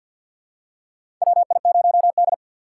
Morse code, heard as a keyed tone, about 48 wpm, 700 Hz. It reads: WI0D
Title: USA Call Signs - 6x Character Spacing - 50wpm